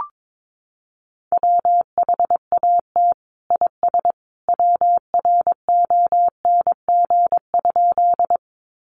Morse code, heard as a keyed tone, about 22 wpm, 700 Hz. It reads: WHAT IS WRONG?
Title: Sentences from Top 900 Words 22wpm